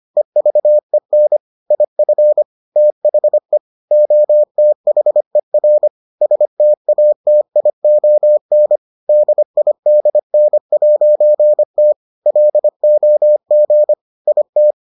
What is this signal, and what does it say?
Morse code, heard as a keyed tone, about 25 wpm, 600 Hz. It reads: EVEN IF THE OTHER STATION DIDN'T LOG IT